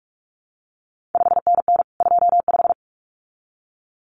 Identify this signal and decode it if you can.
Morse code, heard as a keyed tone, about 46 wpm, 700 Hz. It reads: 5NN 25
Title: CQ WW DX Contest Exchange Encore - 45wpm